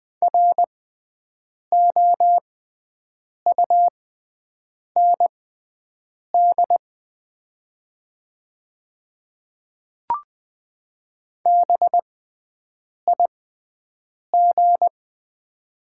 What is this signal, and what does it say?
Morse code, heard as a keyed tone, about 20 wpm, spaced as for 7 wpm, 690 Hz. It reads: ROUND BIG
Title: 6x Character Spacing - Top 200 Words - 20wpm